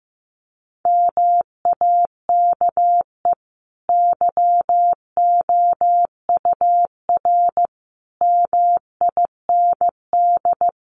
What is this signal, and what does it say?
Morse code, heard as a keyed tone, about 14 wpm, 700 Hz. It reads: MAKE YOUR MIND
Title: Sentences from Top 700 Words 15wpm